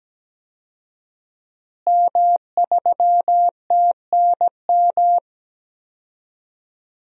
Morse code, heard as a keyed tone, about 17 wpm, 700 Hz. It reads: M3TNM
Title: DXCC-100 Call Signs - 5 Characters - Repeated 2x - 17wpm